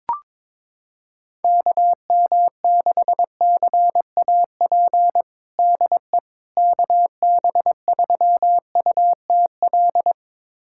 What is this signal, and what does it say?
Morse code, heard as a keyed tone, about 22 wpm, 700 Hz. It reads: KM6CAP DE KB3UTL
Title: USA Callsign DE Callsign 22wpm